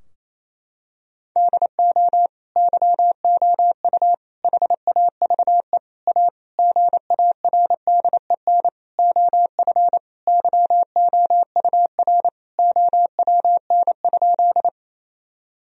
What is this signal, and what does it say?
Morse code, heard as a keyed tone, about 28 wpm, 700 Hz. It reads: DO YOU HAVE A GARDEN OF YOUR OWN?
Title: Sentences from Top 1000 Words - Review Part 4 - 28wpm